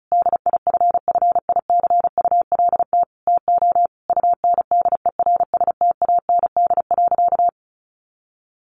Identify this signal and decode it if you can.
Morse code, heard as a keyed tone, about 34 wpm, 700 Hz. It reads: DIFFICULT TO UNDERSTAND.
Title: Sentences from Top 1100 Words 35wpm